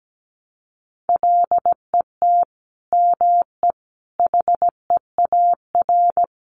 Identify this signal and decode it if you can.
Morse code, heard as a keyed tone, about 17 wpm, 700 Hz. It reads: LET ME HEAR
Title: Sentences from Top 400 Words 17wpm